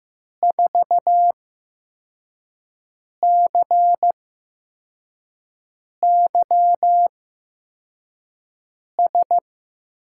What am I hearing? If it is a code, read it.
Morse code, heard as a keyed tone, about 15 wpm, spaced as for 4 wpm, 700 Hz. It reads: 4CYS
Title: USA Call Signs - 6 Characters - 8x Character Spacing - 15wpm